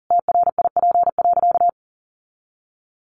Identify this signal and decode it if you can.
Morse code, heard as a keyed tone, about 40 wpm, 700 Hz. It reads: TRIP.
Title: Sentences from Top 1500 Words 40wpm